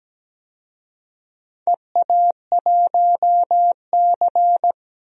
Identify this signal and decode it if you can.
Morse code, heard as a keyed tone, about 17 wpm, 700 Hz. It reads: EA1C
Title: DXCC-100 Call Signs - 4 Characters - 17wpm